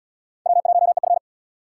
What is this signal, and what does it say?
Morse code, heard as a keyed tone, about 76 wpm, 700 Hz. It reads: W1EF